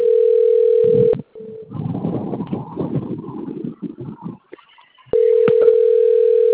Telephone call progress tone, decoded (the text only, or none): ringback tone